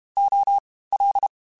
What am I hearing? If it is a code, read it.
Morse code, heard as a keyed tone, about 32 wpm, 800 Hz. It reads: O L
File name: Special INGs / Special INGs-mp3s / Fast ing @30